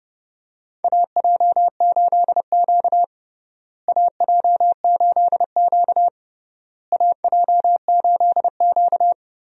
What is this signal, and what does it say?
Morse code, heard as a keyed tone, about 30 wpm, 700 Hz. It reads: AJ8Q AJ8Q AJ8Q
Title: USA Call Signs - 4 Characters - Repeated 3x - 30wpm